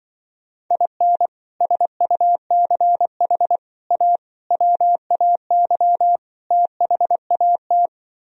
Morse code, heard as a keyed tone, about 24 wpm, 700 Hz. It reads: IN SUCH A WAY THAT
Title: Speed-Racing - Sets of 5 Words 20wpm